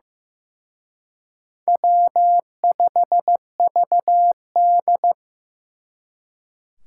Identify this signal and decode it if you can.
Morse code, heard as a keyed tone, about 18 wpm, 700 Hz. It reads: W5VD
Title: Speed Racing - USA Call Signs 15wpm